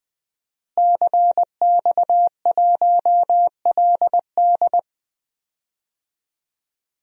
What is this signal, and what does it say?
Morse code, heard as a keyed tone, about 20 wpm, 700 Hz. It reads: CX1LD
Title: DXCC-100 Call Signs - 5 Characters - Repeated 3x - 20wpm